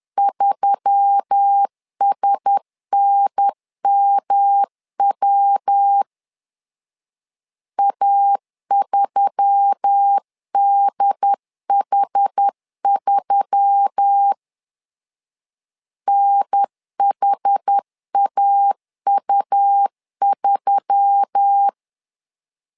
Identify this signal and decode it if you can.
Morse code, heard as a keyed tone, about 10 wpm, 800 Hz. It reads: 3SNMW A3DH3 NHAU3
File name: LEZIONI / 4 Q1S3 10wpm 2427